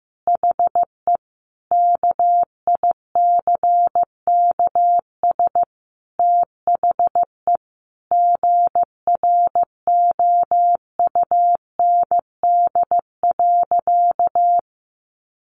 Morse code, heard as a keyed tone, about 15 wpm, 700 Hz. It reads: HE KICKS THE GROUND.
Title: Additional Easy Phrases 15wpm